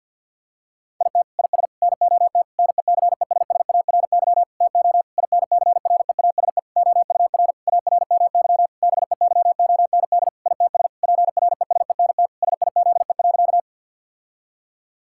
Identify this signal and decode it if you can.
Morse code, heard as a keyed tone, about 50 wpm, 700 Hz. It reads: IT IS NOT NECESSARY TO INCREASE OUR ARMY BEYOND ITS PRESENT SIZE.
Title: Sentences from Top 1000 Words - Review Part 6 - 50wpm